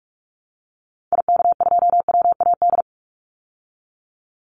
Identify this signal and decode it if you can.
Morse code, heard as a keyed tone, about 44 wpm, 700 Hz. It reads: IK2WAD
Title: Call Signs Repeated 3x - 45wpm